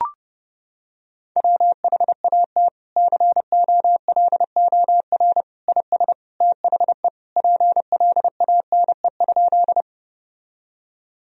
Morse code, heard as a keyed tone, about 31 wpm, 700 Hz. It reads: WHAT COLOR IS THE PLANE?
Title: Sentences from Top 1000 Words - Review Part 3 - 30wpm